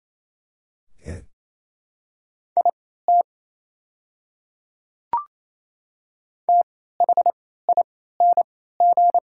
Morse code, heard as a keyed tone, about 27 wpm, spaced as for 16 wpm, 700 Hz. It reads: IT THING
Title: 3x Character Spacing - Top 100 Words - 28wpm